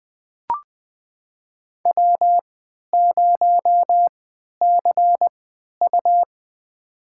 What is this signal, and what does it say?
Morse code, heard as a keyed tone, about 20 wpm, spaced as for 11 wpm, 700 Hz. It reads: W0CU
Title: USA Call Signs - 4 Characters - 3x Character Spacing - 20wpm